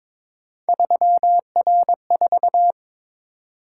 Morse code, heard as a keyed tone, about 26 wpm, 700 Hz. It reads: 3R4
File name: Speed-Racing - Three Letter-Number Combinations - 22wpm